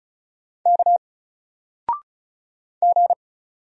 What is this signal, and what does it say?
Morse code, heard as a keyed tone, about 34 wpm, 700 Hz. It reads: K G
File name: Single-Letter-Number-35wpm